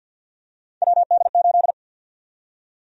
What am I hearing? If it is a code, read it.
Morse code, heard as a keyed tone, about 50 wpm, 700 Hz. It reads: WD8
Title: USA Call Sign Prefixes - 50wpm